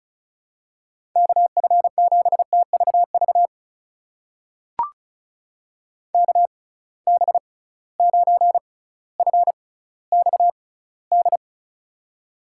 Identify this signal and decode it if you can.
Morse code, heard as a keyed tone, about 34 wpm, spaced as for 18 wpm, 700 Hz. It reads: KF7TVV KB9FXD
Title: USA Call Signs - 6 Characters - 6x Character Spacing - 35wpm